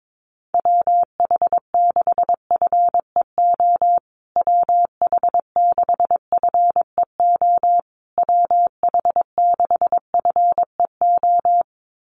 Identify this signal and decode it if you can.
Morse code, heard as a keyed tone, about 22 wpm, 700 Hz. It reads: WH6FEO WH6FEO WH6FEO
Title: USA Call Signs Repeated 3x 22wpm